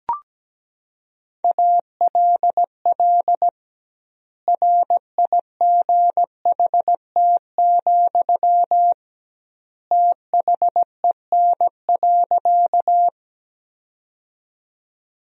Morse code, heard as a keyed tone, about 17 wpm, 700 Hz. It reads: ALL RIGHT, THEN.
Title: 2x Word-Spacing - Sentences from Top 200 Words - 17wpm